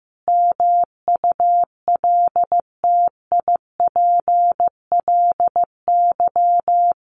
Morse code, heard as a keyed tone, about 15 wpm, 700 Hz. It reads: MULTIPLY
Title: A Sound Of Thunder 15wpm